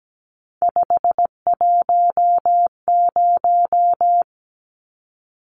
Morse code, heard as a keyed tone, about 17 wpm, 700 Hz. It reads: 510
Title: Numbers - 3-Digit 17wpm